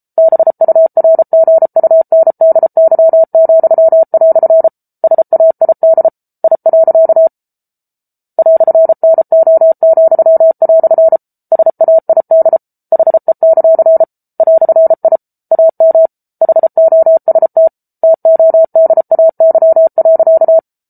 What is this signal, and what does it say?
Morse code, heard as a keyed tone, about 34 wpm, 650 Hz. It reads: BURGUNDY," SAID I. "NO," SAID HE; "I AM HOST TODAY.